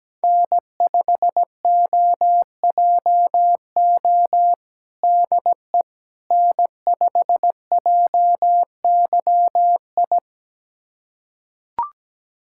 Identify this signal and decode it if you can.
Morse code, heard as a keyed tone, about 17 wpm, 700 Hz. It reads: N5OJO DE N5JYI
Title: USA Callsign DE Callsign - 5 Characters - 17wpm